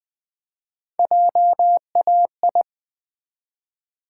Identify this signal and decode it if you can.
Morse code, heard as a keyed tone, about 20 wpm, 700 Hz. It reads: JAI
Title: Build A Word - 1 Letter At A Time - 4-Letter Words - 20wpm